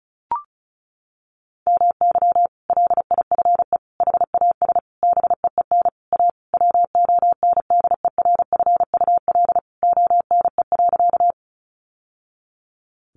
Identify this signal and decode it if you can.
Morse code, heard as a keyed tone, about 34 wpm, 700 Hz. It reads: MY LIFE HAS BEEN A WONDERFUL ONE.
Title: Sentences from Top 1100 Words 35wpm